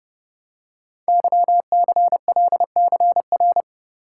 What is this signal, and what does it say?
Morse code, heard as a keyed tone, about 28 wpm, 700 Hz. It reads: YCLCR
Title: Code Groups - 5 Letters - 30wpm